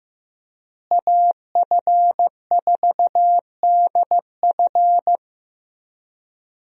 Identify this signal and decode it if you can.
Morse code, heard as a keyed tone, about 15 wpm, 700 Hz. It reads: AF4DF
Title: USA Call Signs - 5 Characters - Repeated 2x - 15wpm